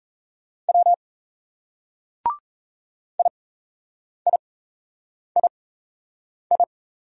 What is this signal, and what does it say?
Morse code, heard as a keyed tone, about 36 wpm, spaced as for 8 wpm, 700 Hz. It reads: W IIII